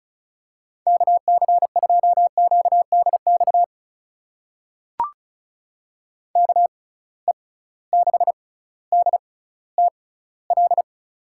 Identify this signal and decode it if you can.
Morse code, heard as a keyed tone, about 36 wpm, spaced as for 18 wpm, 690 Hz. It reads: KC2QDX KE6DTL
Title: USA Call Signs - 6 Characters - 6x Character Spacing - 35wpm